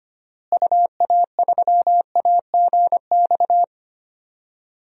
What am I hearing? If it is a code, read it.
Morse code, heard as a keyed tone, about 25 wpm, 700 Hz. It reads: UA3AGX
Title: DXCC-100 Call Signs - 6 Characters - Repeated 3x - 25wpm